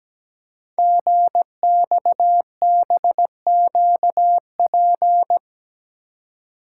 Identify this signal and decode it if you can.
Morse code, heard as a keyed tone, about 17 wpm, 700 Hz. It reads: GXBQP
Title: Code Groups - 5 Letters - 17wpm